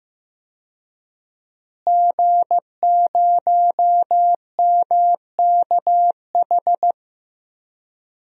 Morse code, heard as a keyed tone, about 15 wpm, 700 Hz. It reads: G0MKH
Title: DXCC-100 Call Signs - 5 Characters - Repeated 2x - 15wpm